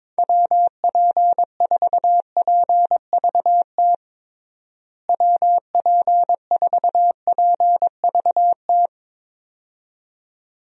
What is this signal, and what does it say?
Morse code, heard as a keyed tone, about 22 wpm, 700 Hz. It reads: WP4PVT WP4PVT
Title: USA Call Signs - 6 Characters - Repeated 2x - 22wpm